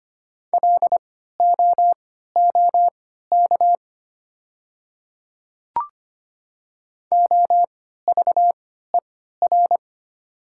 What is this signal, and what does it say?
Morse code, heard as a keyed tone, about 25 wpm, spaced as for 14 wpm, 700 Hz. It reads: LOOK OVER
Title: 3x Character Spacing - Top 100 Words - 25wpm